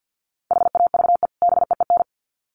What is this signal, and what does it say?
Morse code, heard as a keyed tone, about 50 wpm, 700 Hz. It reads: HAVE BEEN
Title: Sets of 2 Words 50wpm